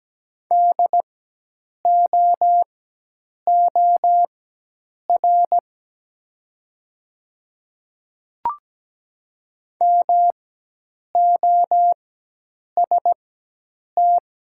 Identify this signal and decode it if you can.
Morse code, heard as a keyed tone, about 17 wpm, spaced as for 8 wpm, 700 Hz. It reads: DOOR MOST
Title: 4x Character Spacing - Top 300 Words - 17wpm